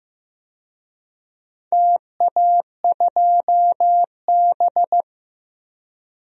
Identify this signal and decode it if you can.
Morse code, heard as a keyed tone, about 15 wpm, 700 Hz. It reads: TA2B
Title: DXCC-100 Call Signs - 4 Characters - Repeated 2x - 15wpm